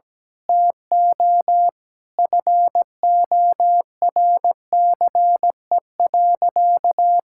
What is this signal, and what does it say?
Morse code, heard as a keyed tone, about 17 wpm, 700 Hz. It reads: TO FORCE.
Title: Sentences from Top 1000 Words - Review Part 6 - 17wpm